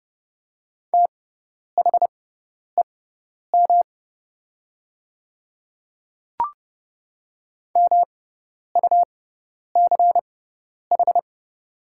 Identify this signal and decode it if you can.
Morse code, heard as a keyed tone, about 30 wpm, spaced as for 10 wpm, 700 Hz. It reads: THEM MUCH
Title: 6x Character Spacing - Top 300 Words - 30wpm